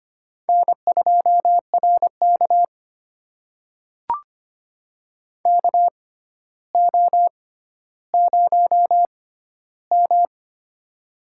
Morse code, heard as a keyed tone, about 25 wpm, spaced as for 13 wpm, 700 Hz. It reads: N2RK KO0M
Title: USA Call Signs - 4 Characters - 6x Character Spacing - 25wpm